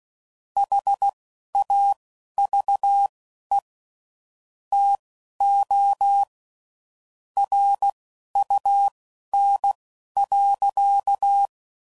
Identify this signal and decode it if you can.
Morse code, heard as a keyed tone, about 16 wpm, spaced as for 11 wpm, 800 Hz. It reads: HAVE TO RUN.
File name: CW Code Groups @10WPM / QSOs / QSO kc7haa